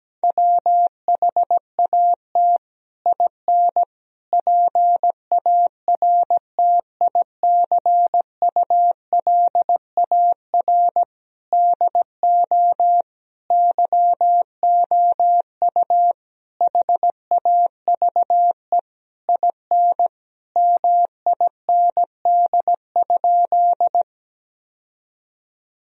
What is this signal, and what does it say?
Morse code, heard as a keyed tone, about 17 wpm, 700 Hz. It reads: WHAT IN PARTICULAR DO YOU HAVE IN MIND?
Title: Sentences from Top 1000 Words - Review Part 4 - 17wpm